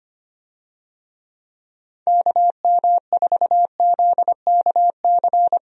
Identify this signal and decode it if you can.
Morse code, heard as a keyed tone, about 25 wpm, 700 Hz. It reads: KM4ZKC
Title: USA Call Signs - 6 Characters - 25wpm